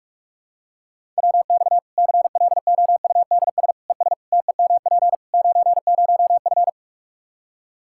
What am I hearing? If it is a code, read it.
Morse code, heard as a keyed tone, about 46 wpm, 700 Hz. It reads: WX CLOUDS ES TEMP 90F
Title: Practice QSO - Exchange 2 - Info - 45wpm